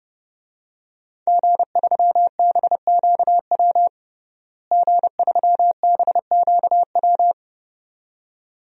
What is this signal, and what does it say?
Morse code, heard as a keyed tone, about 30 wpm, 700 Hz. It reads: G3BQW G3BQW
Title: DXCC-100 Call Signs - 5 Characters - Repeated 2x - 30wpm